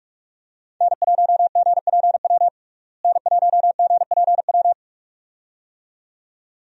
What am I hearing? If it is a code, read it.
Morse code, heard as a keyed tone, about 46 wpm, 700 Hz. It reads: N1GPW N1GPW
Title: USA Call Signs - 5 Characters - Repeated 2x - 45wpm